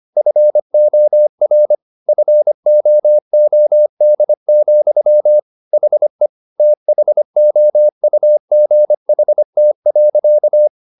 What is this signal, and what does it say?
Morse code, heard as a keyed tone, about 25 wpm, 600 Hz. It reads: FOR FOOD, HE THOUGHT.